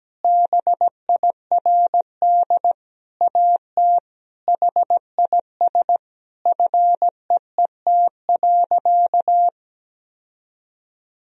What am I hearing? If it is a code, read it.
Morse code, heard as a keyed tone, about 17 wpm, 700 Hz. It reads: BIRD AT HIS FEET.
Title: Sentences from Top 1000 Words - Review Part 4 - 17wpm